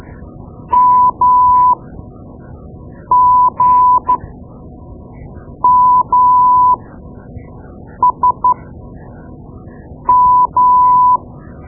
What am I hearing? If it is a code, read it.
Morse code, heard as a keyed tone, about 13 wpm, spaced as for 5 wpm, 1000 Hz. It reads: MGMSM
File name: Morse Lessons / Morse Lessons / Tape-2B